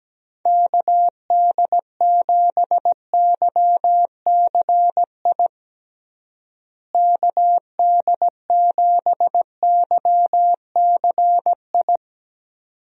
Morse code, heard as a keyed tone, about 17 wpm, 700 Hz. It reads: KD7YCI KD7YCI
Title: USA Call Signs - 6 Characters - Repeated 2x - 17wpm